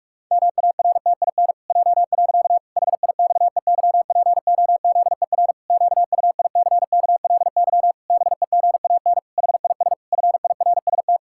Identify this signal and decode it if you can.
Morse code, heard as a keyed tone, about 46 wpm, 700 Hz. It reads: MARTIN J. HIXEYPOZER QUICKLY BEGAN HIS FIRST
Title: Pangram Challenge v2 - 45wpm